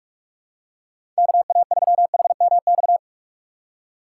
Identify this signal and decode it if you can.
Morse code, heard as a keyed tone, about 44 wpm, 700 Hz. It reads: K A 3 H M X
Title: USA Call Signs - 6 Characters - 6x Character Spacing - 45wpm